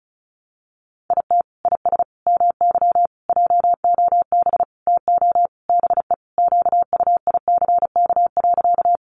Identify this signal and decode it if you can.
Morse code, heard as a keyed tone, about 34 wpm, 700 Hz. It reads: IT IS MY JOB TO BE QUICK.
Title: Sentences from Top 600 Words 35wpm